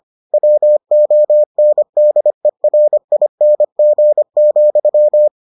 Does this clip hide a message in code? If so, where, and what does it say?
Yes: Morse code from 0.3 to 5.3 s, heard as a keyed tone, about 25 wpm, 600 Hz. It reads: WONDERING,